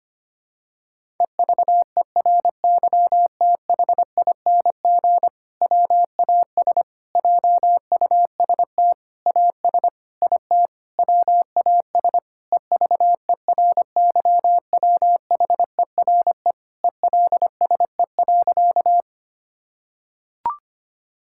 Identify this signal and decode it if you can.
Morse code, heard as a keyed tone, about 25 wpm, 700 Hz. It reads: EVERYTHING WAS JUST AS IT WAS EVERYWHERE ELSE.
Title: Sentences from Top 1600 Words 25wpm